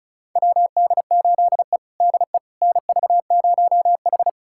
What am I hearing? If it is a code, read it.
Morse code, heard as a keyed tone, about 36 wpm, 700 Hz. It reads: WD8E DE NV0H
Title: USA Callsign DE Callsign - 4 Characters - 35wpm